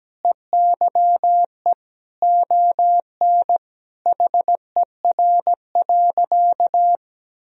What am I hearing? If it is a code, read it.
Morse code, heard as a keyed tone, about 17 wpm, 700 Hz. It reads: EYE ON HER.
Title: Sentences from Top 1000 Words - Review Part 1 - 17wpm